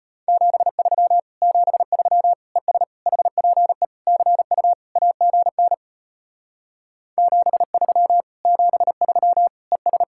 Morse code, heard as a keyed tone, about 33 wpm, 700 Hz. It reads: 73 73 ES HPE CU AGN 73 73 ES